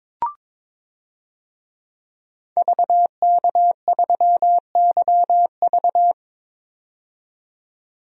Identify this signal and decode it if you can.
Morse code, heard as a keyed tone, about 21 wpm, 700 Hz. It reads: VK3YV
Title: DXCC-100 Call Signs - 5 Characters - 22wpm